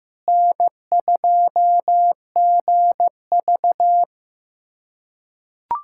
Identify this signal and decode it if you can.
Morse code, heard as a keyed tone, about 16 wpm, 700 Hz. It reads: N2GV
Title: USA Callsign DE Callsign - 4 Characters - 15wpm